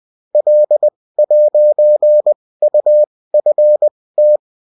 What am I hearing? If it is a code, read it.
Morse code, heard as a keyed tone, about 20 wpm, spaced as for 16 wpm, 600 Hz. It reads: L'UFT